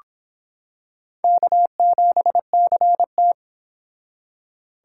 Morse code, heard as a keyed tone, about 26 wpm, 700 Hz. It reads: K7CT